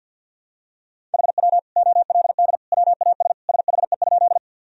Morse code, heard as a keyed tone, about 53 wpm, 700 Hz. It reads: HW OLD WAS SHE?
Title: Sentences from Top 200 Words and Abbreviations 50wpm